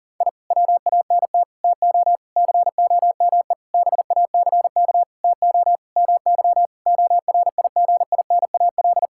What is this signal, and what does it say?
Morse code, heard as a keyed tone, about 40 wpm, 700 Hz. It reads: I WANT TO COME BACK TO MY ORIGINAL
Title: Sentences from Top 1000 Words - Review Part 6 - 40wpm